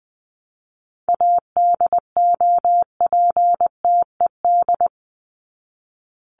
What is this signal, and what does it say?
Morse code, heard as a keyed tone, about 20 wpm, 700 Hz. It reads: ADOPTED
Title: ED Suffix - Top 100 Words with Base Words 20wpm